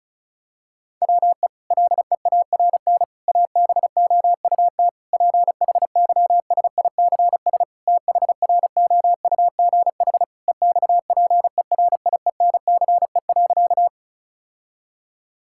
Morse code, heard as a keyed tone, about 36 wpm, 700 Hz. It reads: WE LEARN ABOUT PHYSICS THROUGH EXPERIENCE.
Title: Sentences from Top 1000 Words - Review Part 5 - 35wpm